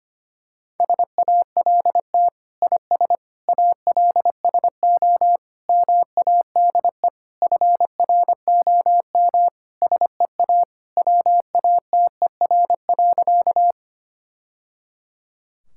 Morse code, heard as a keyed tone, about 25 wpm, 700 Hz. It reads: SALT IS ALSO MADE FROM SEA WATER.
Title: Sentences from Top 1000 Words - Review Part 5 - 25wpm